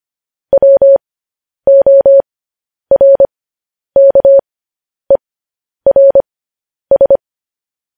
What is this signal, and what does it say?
Morse code, heard as a keyed tone, about 25 wpm, spaced as for 10 wpm, 550 Hz. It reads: WORKERS